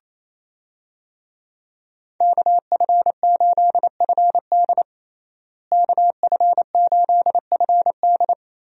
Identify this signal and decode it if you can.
Morse code, heard as a keyed tone, about 27 wpm, 700 Hz. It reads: KF8FD KF8FD
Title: USA Call Signs - 5 Characters - Repeated 3x - 28wpm